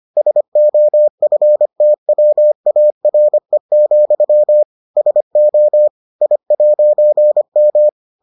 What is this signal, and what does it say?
Morse code, heard as a keyed tone, about 25 wpm, 600 Hz. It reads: SOFTWARE, SO I'M